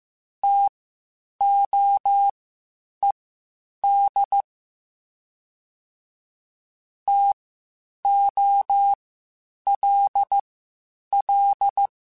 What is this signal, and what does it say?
Morse code, heard as a keyed tone, about 15 wpm, spaced as for 8 wpm, 800 Hz. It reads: TOED TOLL